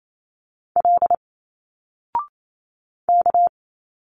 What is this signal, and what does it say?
Morse code, heard as a keyed tone, about 28 wpm, 700 Hz. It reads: L K
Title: Single Letters 28wpm